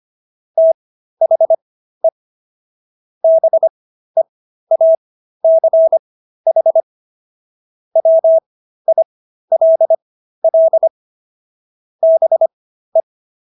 Morse code, heard as a keyed tone, about 25 wpm, spaced as for 13 wpm, 650 Hz. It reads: THE BEACH WILL BE